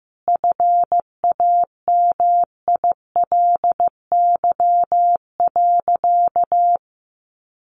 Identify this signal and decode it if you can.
Morse code, heard as a keyed tone, about 15 wpm, 700 Hz. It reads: FAMILY.
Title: Sentences from Top 1300 Words 15wpm